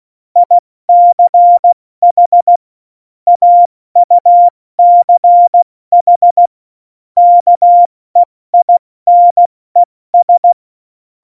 Morse code, heard as a keyed tone, about 16 wpm, 700 Hz. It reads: ICH AUCH KEINES
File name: Morsekurs / Witze / Witz 10.Dezember 2019, 70 F80